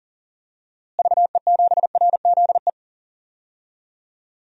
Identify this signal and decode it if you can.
Morse code, heard as a keyed tone, about 40 wpm, 700 Hz. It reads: VE7RZE
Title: DXCC-100 Call Signs - 6 Characters - Repeated 3x - 40wpm